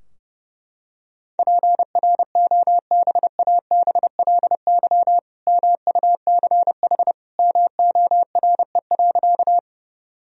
Morse code, heard as a keyed tone, about 30 wpm, 700 Hz. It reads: PROBABLY MUCH MORE.
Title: Sentences from Top 1000 Words - Review Part 4 - 30wpm